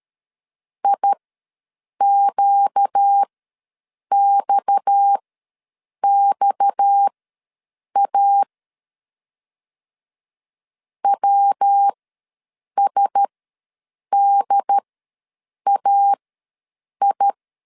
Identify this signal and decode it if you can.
Morse code, heard as a keyed tone, about 13 wpm, spaced as for 7 wpm, 800 Hz. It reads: IQXXA WSDAI
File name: LEZIONI / 6 PKX0 / 6 PKX ws 12wpm 3106